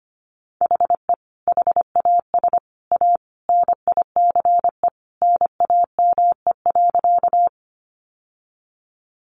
Morse code, heard as a keyed tone, about 25 wpm, 700 Hz. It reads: HE HAS A NICE NAME.
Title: Sentences from Top 1100 Words 25wpm